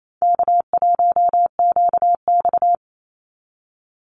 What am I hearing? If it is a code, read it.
Morse code, heard as a keyed tone, about 28 wpm, 700 Hz. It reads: K1QX
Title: Easy Call Signs Repeated 3x - 28wpm